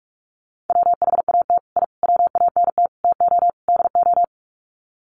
Morse code, heard as a keyed tone, about 44 wpm, 700 Hz. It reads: WHAT I WANT TO DO
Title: Sets of 5 Words Encore 45wpm